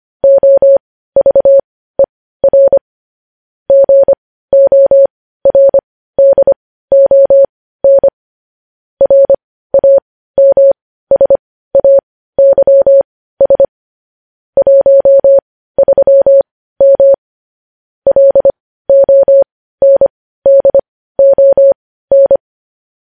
Morse code, heard as a keyed tone, about 25 wpm, spaced as for 15 wpm, 550 Hz. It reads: OVER GORDON RAMSAYS 13M LONDON
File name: podcast / News headlines-1x-25x15